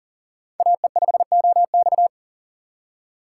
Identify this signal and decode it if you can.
Morse code, heard as a keyed tone, about 40 wpm, 700 Hz. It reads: AE5OX
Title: USA Call Signs - 6x Character Spacing - 40wpm